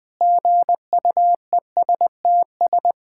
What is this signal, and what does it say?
Morse code, heard as a keyed tone, about 19 wpm, 700 Hz. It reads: GUESTS
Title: Sentences with 6-Word Phrases - 20wpm